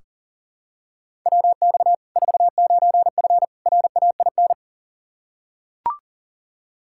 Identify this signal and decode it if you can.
Morse code, heard as a keyed tone, about 40 wpm, 700 Hz. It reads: WX 49F RAIN
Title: Practice QSO - Exchange 2 - Info - 40wpm